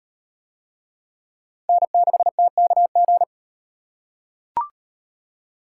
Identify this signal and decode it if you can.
Morse code, heard as a keyed tone, about 36 wpm, 700 Hz. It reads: N6TKG